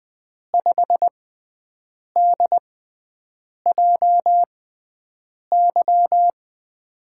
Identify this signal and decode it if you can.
Morse code, heard as a keyed tone, about 20 wpm, spaced as for 7 wpm, 700 Hz. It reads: 5DJY
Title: USA Call Signs - 6x Character Spacing - 20wpm